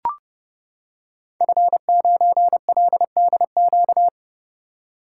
Morse code, heard as a keyed tone, about 30 wpm, 700 Hz. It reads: F9LDQ
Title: Code Groups - 5 Letter:Numbers - 30wpm